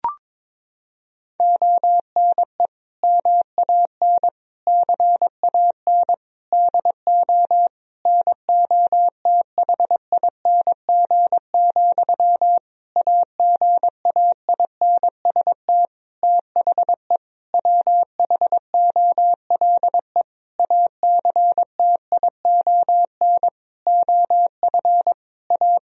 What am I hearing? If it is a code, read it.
Morse code, heard as a keyed tone, about 22 wpm, 700 Hz. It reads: ONE MAN CAN DO NOTHING, AGAINST THE WHOLE ACTION OF A
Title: Sentences from Top 1000 Words - Review Part 4 - 22wpm